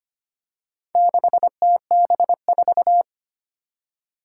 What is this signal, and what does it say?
Morse code, heard as a keyed tone, about 25 wpm, 700 Hz. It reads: 6TB4
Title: Four Challenging Letter-Number Combinations 25wpm